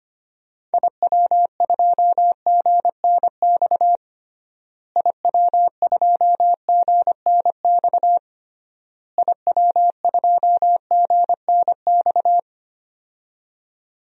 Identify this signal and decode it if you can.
Morse code, heard as a keyed tone, about 25 wpm, 700 Hz. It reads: IW2GNX IW2GNX IW2GNX
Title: DXCC-100 Call Signs - 6 Characters - Repeated 3x - 25wpm